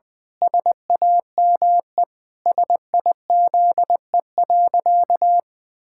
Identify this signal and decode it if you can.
Morse code, heard as a keyed tone, about 20 wpm, 700 Hz. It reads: SAME SIZE.
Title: Pangram Challenge v2 - 20wpm